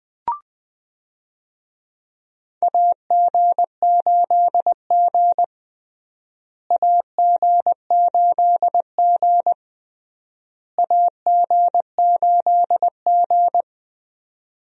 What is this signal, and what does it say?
Morse code, heard as a keyed tone, about 20 wpm, 700 Hz. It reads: AG8G AG8G AG8G
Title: USA Call Signs - 4 Characters - Repeated 3x - 20wpm